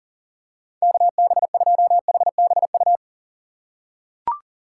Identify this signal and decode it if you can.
Morse code, heard as a keyed tone, about 37 wpm, 700 Hz. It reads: KB2HBU
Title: USA Call Signs - 6 Characters - 3x Character Spacing - 40wpm